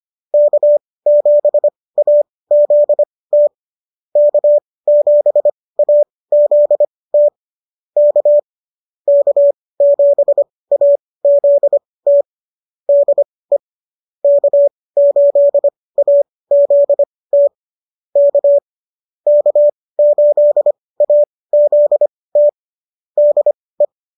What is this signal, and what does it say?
Morse code, heard as a keyed tone, about 25 wpm, spaced as for 18 wpm, 590 Hz. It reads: K7AZT K7AZT K K7AZT DE K8AZT K K8AZT DE